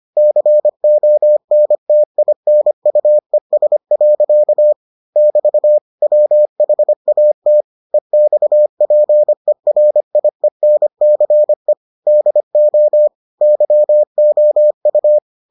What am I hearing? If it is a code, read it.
Morse code, heard as a keyed tone, about 25 wpm, 600 Hz. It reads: CONTINUES. <BT> WHAT EXPERIENCE DO YOU